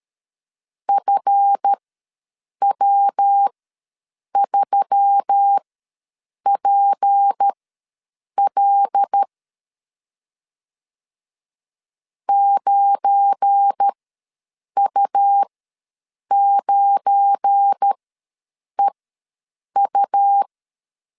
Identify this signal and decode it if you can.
Morse code, heard as a keyed tone, about 13 wpm, spaced as for 7 wpm, 800 Hz. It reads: FW3PL 9U9EU
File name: LEZIONI / 7 GF9Z / 7 GF9 ws 12wpm 3344